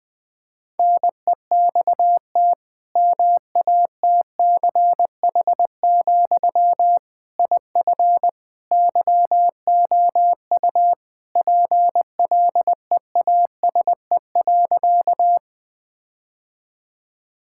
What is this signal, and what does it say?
Morse code, heard as a keyed tone, about 20 wpm, 700 Hz. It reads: NEXT MATCH, IF YOU PLEASE.
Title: Sentences from Top 1600 Words 20wpm